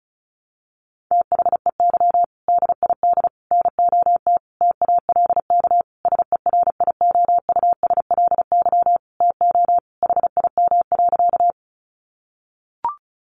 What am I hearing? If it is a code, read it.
Morse code, heard as a keyed tone, about 34 wpm, 700 Hz. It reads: THEY DID NOT TALK SERIOUSLY TO HIM.
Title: Sentences from Top 1200 Words 35wpm